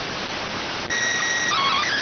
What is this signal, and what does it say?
SSTV (slow-scan television), PD 290.